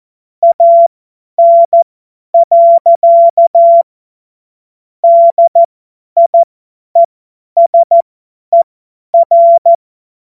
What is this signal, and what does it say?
Morse code, heard as a keyed tone, about 14 wpm, spaced as for 10 wpm, 680 Hz. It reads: AN. DIESER
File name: Morsekurs / Witze / Witz 19. Oktober 2021, 50 F70